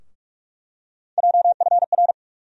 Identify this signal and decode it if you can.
Morse code, heard as a keyed tone, about 46 wpm, 700 Hz. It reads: JFR
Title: USA Call Sign Postfixes - 3 Letters 45wpm